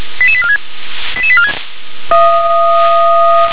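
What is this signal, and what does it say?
Five-tone selective call (ZVEI), heard as quick runs of five tones, tones of about 70 ms. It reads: 88846 88846
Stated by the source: ZVEI five-tone selective call 88100, received by SDR.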